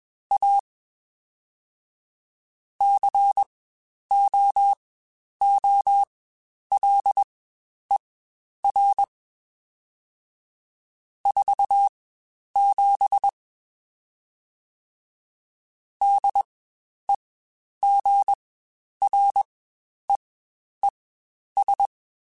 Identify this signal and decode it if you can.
Morse code, heard as a keyed tone, about 21 wpm, spaced as for 10 wpm, 800 Hz. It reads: A COOLER 47 DEGREES
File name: Spaced Characters / Spaced Characters-mp3s / Weather5 5 sp @20